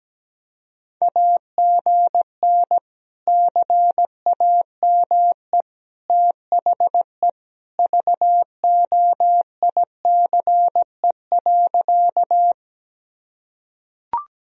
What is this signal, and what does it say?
Morse code, heard as a keyed tone, about 17 wpm, 700 Hz. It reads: AGN CAME THE VOICE.
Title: Sentences from Top 500 Words and Abbreviations 17wpm